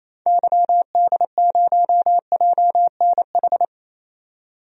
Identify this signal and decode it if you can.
Morse code, heard as a keyed tone, about 28 wpm, 700 Hz. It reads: YD0JNH
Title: DXCC-100 Call Signs - 6 Characters - 28wpm